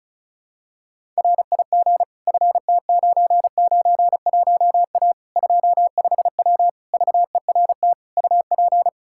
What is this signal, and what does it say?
Morse code, heard as a keyed tone, about 36 wpm, 700 Hz. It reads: RIG FT991A 25W VERT UP
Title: Practice QSO - Exchange 2 - Info - 35wpm